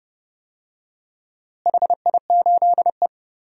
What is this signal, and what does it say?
Morse code, heard as a keyed tone, about 30 wpm, 700 Hz. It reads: HI8E
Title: DXCC-100 Call Signs - 4 Characters - Repeated 2x - 30wpm